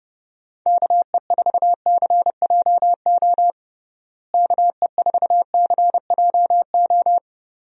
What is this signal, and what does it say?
Morse code, heard as a keyed tone, about 32 wpm, 700 Hz. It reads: KE4CJO KE4CJO
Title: USA Call Signs - 6 Characters - Repeated 2x - 30wpm